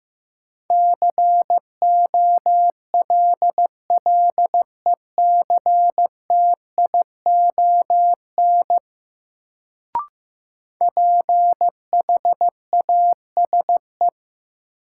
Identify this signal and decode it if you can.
Morse code, heard as a keyed tone, about 15 wpm, 700 Hz. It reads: COLLECTION PHASE
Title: Sentences from Top 1200 Words - New Word Drill - 15wpm